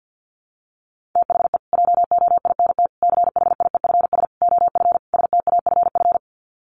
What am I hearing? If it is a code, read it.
Morse code, heard as a keyed tone, about 48 wpm, 700 Hz. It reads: THE JOINT CHIEFS OF STAFF
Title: Sets of 5 Words Encore2 50wpm